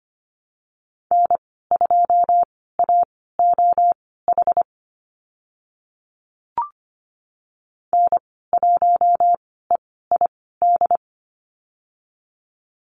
Morse code, heard as a keyed tone, about 25 wpm, spaced as for 16 wpm, 690 Hz. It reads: N2AOH N1EID
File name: Lesson 15g - Number 5 - Call Signs 25wpm